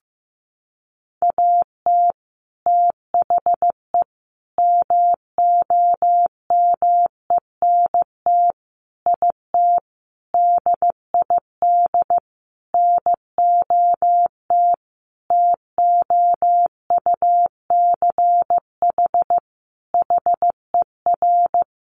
Morse code, heard as a keyed tone, about 15 wpm, 700 Hz. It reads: AT THE MOMENT IT DID NOT TOUCH HER
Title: Sentences from Top 1200 Words 15wpm